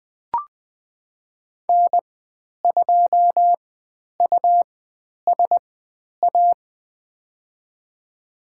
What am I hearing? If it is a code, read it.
Morse code, heard as a keyed tone, about 20 wpm, spaced as for 10 wpm, 700 Hz. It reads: N2USA
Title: 109 Lesson 18g - Letter m - Call Signs 20wpm